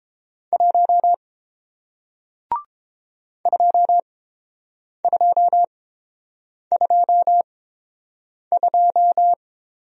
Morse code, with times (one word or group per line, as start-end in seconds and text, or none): 0.5-1.1 1
3.5-9.3 2222